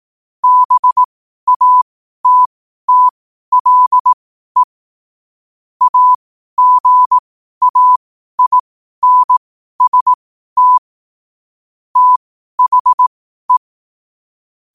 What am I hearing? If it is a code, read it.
Morse code, heard as a keyed tone, about 18 wpm, spaced as for 12 wpm, 1000 Hz. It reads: BATTLE AGAINST THE
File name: qotdfiles / 12wpm